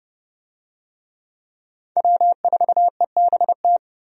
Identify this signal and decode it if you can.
Morse code, heard as a keyed tone, about 30 wpm, 700 Hz. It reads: W4EBT
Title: USA Call Signs - 5 Characters - Repeated 3x - 30wpm